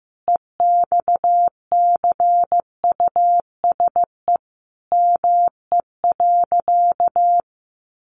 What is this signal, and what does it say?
Morse code, heard as a keyed tone, about 15 wpm, 700 Hz. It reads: EXCUSE ME.
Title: Sentences from Top 700 Words 15wpm